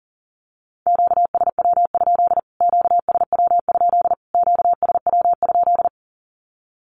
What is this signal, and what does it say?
Morse code, heard as a keyed tone, about 40 wpm, 700 Hz. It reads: QSW? QSW? QSW?
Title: Q-Codes Repeated 3x 40wpm